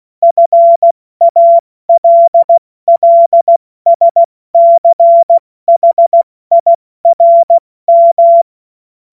Morse code, heard as a keyed tone, about 16 wpm, 680 Hz. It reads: FALLSCHIRM
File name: Morsekurs / Witze / Witz 02. Dezember 2025 70 F80